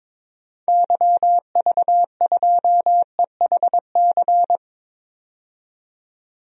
Morse code, heard as a keyed tone, about 22 wpm, 700 Hz. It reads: YV2EHC
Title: DXCC-100 Call Signs - 6 Characters - 22wpm